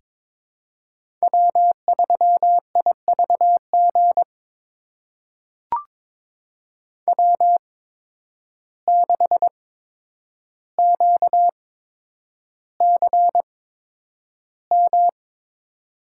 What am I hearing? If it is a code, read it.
Morse code, heard as a keyed tone, about 22 wpm, spaced as for 9 wpm, 700 Hz. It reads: W3IVG W6QCM
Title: USA Call Signs - 8x Character Spacing - 22wpm